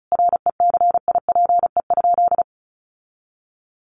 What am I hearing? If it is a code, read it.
Morse code, heard as a keyed tone, about 34 wpm, 700 Hz. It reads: RECIPE?
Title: Sentences from Top 1200 Words 35wpm